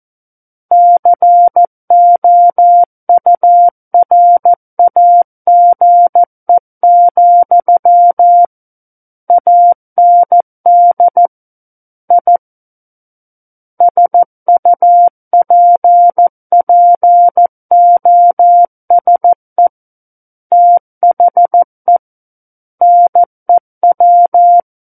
Morse code, heard as a keyed tone, about 14 wpm, 700 Hz. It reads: COURAGE, AND I SUPPOSE THE NEW
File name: Book One / 05-Chapter 5 - 14 WPM